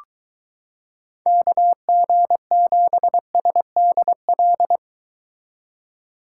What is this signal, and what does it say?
Morse code, heard as a keyed tone, about 23 wpm, 700 Hz. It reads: KG7SDL